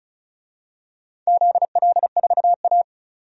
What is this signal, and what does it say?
Morse code, heard as a keyed tone, about 36 wpm, 700 Hz. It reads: ZL4A
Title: DXCC-100 Call Signs - 4 Characters - 35wpm